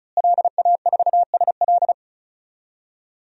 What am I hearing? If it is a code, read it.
Morse code, heard as a keyed tone, about 36 wpm, 690 Hz. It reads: LA4SL
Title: DXCC-100 Call Signs - 5 Characters - Repeated 3x - 35wpm